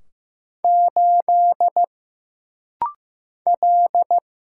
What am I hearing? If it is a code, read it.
Morse code, heard as a keyed tone, about 15 wpm, 700 Hz. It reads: 8 L
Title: Single Letter-Number Alt - Rapid-Fire 15wpm